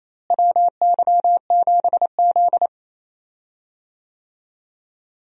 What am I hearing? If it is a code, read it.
Morse code, heard as a keyed tone, about 28 wpm, 700 Hz. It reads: WY7Z
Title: USA Call Signs - 4 Characters - Repeated 2x - 28wpm